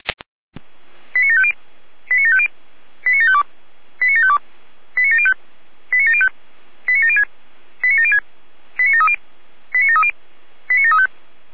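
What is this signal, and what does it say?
Five-tone selective call (ZVEI), heard as quick runs of five tones, tones of about 70 ms. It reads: 89750 89750 89752 89752 89705 89705 89706 89706 89730 89730 89735
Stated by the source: ZVEI five-tone selective call 89533 received with an SDR.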